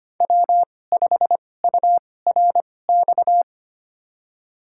Morse code, heard as a keyed tone, about 25 wpm, spaced as for 18 wpm, 700 Hz. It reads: W5URX
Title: USA Call Signs - 2x Character Spacing - 25wpm